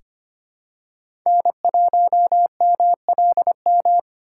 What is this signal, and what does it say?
Morse code, heard as a keyed tone, about 25 wpm, 700 Hz. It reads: N1MLM
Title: USA Call Signs - 4x Character Spacing - 25wpm